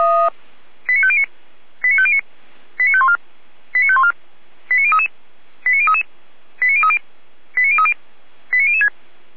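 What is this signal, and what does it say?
Five-tone selective call (ZVEI), heard as quick runs of five tones, tones of about 70 ms. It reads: 89509 89509 89514 89514 89033 89033 89030 89030 89006
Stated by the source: ZVEI five-tone selective call 89099 received by SDR, from the feuerwehr-orlamuende fire brigade audio archive.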